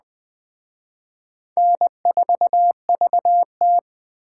Morse code, heard as a keyed tone, about 20 wpm, 700 Hz. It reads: N4VT
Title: Speed Racing - USA 1x2 Call Signs 17wpm